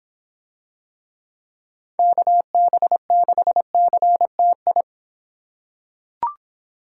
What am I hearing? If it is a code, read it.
Morse code, heard as a keyed tone, about 25 wpm, 700 Hz. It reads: KB6CTI